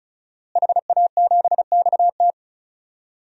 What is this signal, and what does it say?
Morse code, heard as a keyed tone, about 34 wpm, 700 Hz. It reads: HA7XT
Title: DXCC-100 Call Signs - 5 Characters - Repeated 2x - 35wpm